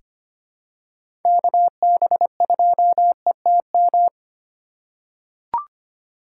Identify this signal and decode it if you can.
Morse code, heard as a keyed tone, about 25 wpm, 700 Hz. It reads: KB2ETM
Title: USA Call Signs - 6 Characters - 3x Character Spacing - 25wpm